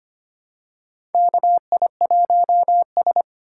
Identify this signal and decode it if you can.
Morse code, heard as a keyed tone, about 25 wpm, 690 Hz. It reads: KI1S
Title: USA Call Signs - 4 Characters - 4x Character Spacing - 25wpm